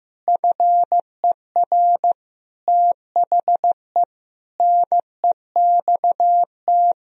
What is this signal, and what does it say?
Morse code, heard as a keyed tone, about 16 wpm, 700 Hz. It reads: FER THE NEXT
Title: Sentences from Top 400 Words and Abbreviations 15wpm